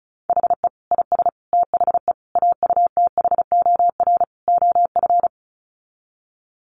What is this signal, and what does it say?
Morse code, heard as a keyed tone, about 34 wpm, 700 Hz. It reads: HE IS THE AUTHOR OF
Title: Sets of 5 Words Encore2 35wpm